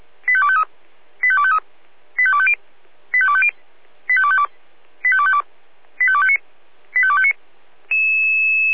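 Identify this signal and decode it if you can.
Five-tone selective call (ZVEI), heard as quick runs of five tones, tones of about 70 ms. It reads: 85253 85253 85250 85250 85252 85252 85259 85259
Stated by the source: ZVEI five-tone selective call 25406 received by SDR.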